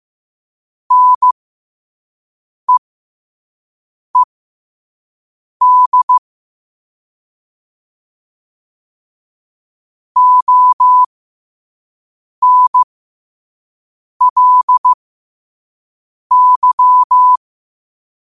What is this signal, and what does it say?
Morse code, heard as a keyed tone, about 15 wpm, spaced as for 5 wpm, 1000 Hz. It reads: NEED ONLY
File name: qotdfiles / 5wpm